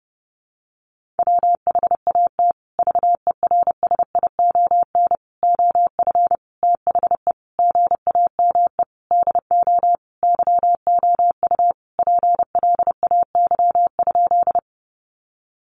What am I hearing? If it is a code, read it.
Morse code, heard as a keyed tone, about 30 wpm, 700 Hz. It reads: WHAT VERSION OF THE GAME DO YOU PLAY?
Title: Sentences from Top 1400 Words 30wpm